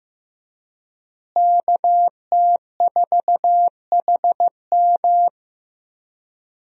Morse code, heard as a keyed tone, about 16 wpm, 700 Hz. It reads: KT4HM
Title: USA Call Signs - 5 Characters - Repeated 2x - 15wpm